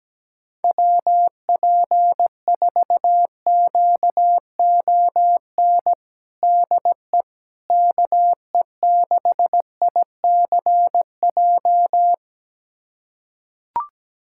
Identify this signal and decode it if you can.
Morse code, heard as a keyed tone, about 17 wpm, 700 Hz. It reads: WP4QON DE KE6ICJ
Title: USA Callsign DE Callsign 17wpm